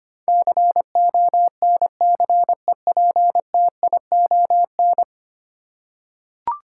Morse code, heard as a keyed tone, about 25 wpm, 700 Hz. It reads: CONCEPTION
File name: 10 Letter Words - First 3 Letters Given - 25wpm